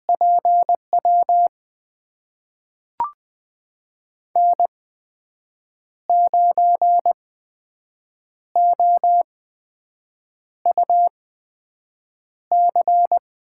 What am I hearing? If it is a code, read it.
Morse code, heard as a keyed tone, about 20 wpm, spaced as for 6 wpm, 700 Hz. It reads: PW N9OUC
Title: USA Call Signs - 5 Characters - 8x Character Spacing - 20wpm